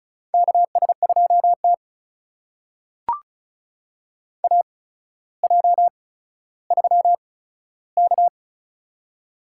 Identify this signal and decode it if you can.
Morse code, heard as a keyed tone, about 36 wpm, spaced as for 15 wpm, 700 Hz. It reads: KS2T AJ3K
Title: USA Call Signs - 4 Characters - 8x Character Spacing - 35wpm